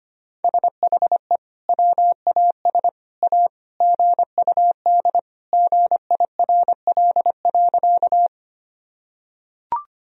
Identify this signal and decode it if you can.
Morse code, heard as a keyed tone, about 25 wpm, 700 Hz. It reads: SHE WAS A GUD GIRL.
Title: Sentences from Top 400 Words and Abbreviations 25wpm